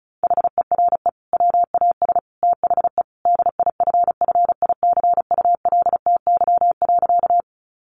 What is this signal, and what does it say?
Morse code, heard as a keyed tone, about 34 wpm, 700 Hz. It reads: HERE WAS THE DIFFICULTY.
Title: Sentences from Top 1500 Words 35wpm